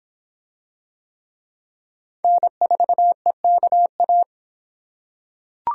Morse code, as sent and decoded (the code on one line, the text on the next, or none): -. ....- . -.- .-
N4EKA